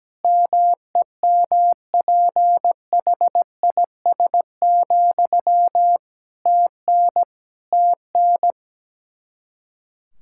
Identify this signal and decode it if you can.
Morse code, heard as a keyed tone, about 17 wpm, 700 Hz. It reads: MEMPHIS, TN TN
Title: Practice QSO - Exchange 1 - Beginning - 17wpm